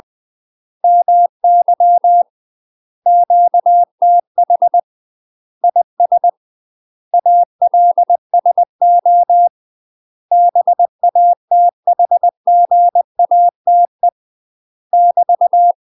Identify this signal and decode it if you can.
Morse code, heard as a keyed tone, about 20 wpm, 700 Hz. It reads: MY QTH IS ALSO BATHGATE <BT>
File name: UK MASTER QSOs TRAINING/20WPM